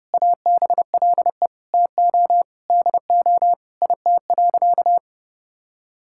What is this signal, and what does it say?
Morse code, heard as a keyed tone, about 30 wpm, 700 Hz. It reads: ABLE TO DO IT.
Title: Sentences from Top 1000 Words - Review Part 6 - 30wpm